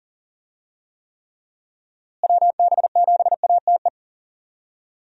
Morse code, heard as a keyed tone, about 40 wpm, 700 Hz. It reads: WB7ATE
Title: USA Call Signs - 6 Characters - Repeated 2x - 40wpm